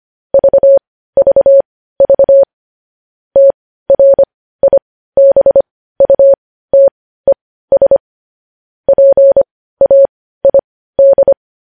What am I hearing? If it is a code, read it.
Morse code, heard as a keyed tone, about 25 wpm, spaced as for 15 wpm, 550 Hz. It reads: VVV TRIBUTES PAID
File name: podcast / News headlines-1x-25x15